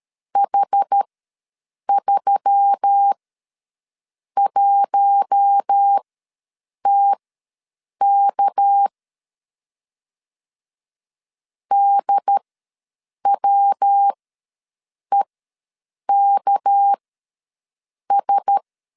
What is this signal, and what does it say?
Morse code, heard as a keyed tone, about 13 wpm, spaced as for 7 wpm, 800 Hz. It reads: H31TK DWEKS
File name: LEZIONI / 6 PKX0 / 6 PK ws 12wpm 3116